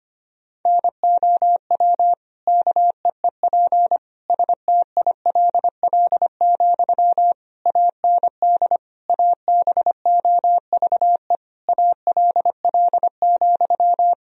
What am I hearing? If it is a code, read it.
Morse code, heard as a keyed tone, about 25 wpm, 700 Hz. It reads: NOW KEEP STILL, AND ABOVE ALL,
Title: Sentences from Top 1000 Words - Review Part 2 - 25wpm